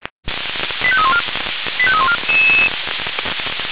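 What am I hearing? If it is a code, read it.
Five-tone selective call (ZVEI), heard as quick runs of five tones, tones of about 70 ms. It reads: 85314 85314
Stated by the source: ZVEI five-tone selective call received on SDR.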